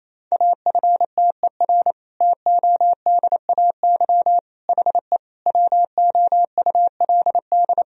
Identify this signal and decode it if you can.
Morse code, heard as a keyed tone, about 27 wpm, 700 Hz. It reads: AFTER TODAY HE WOULD
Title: Sentences from Top 1000 Words - Review Part 4 - 28wpm